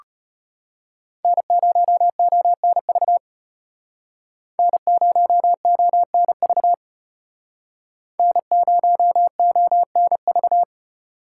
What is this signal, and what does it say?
Morse code, heard as a keyed tone, about 34 wpm, 700 Hz. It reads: N0ONV N0ONV N0ONV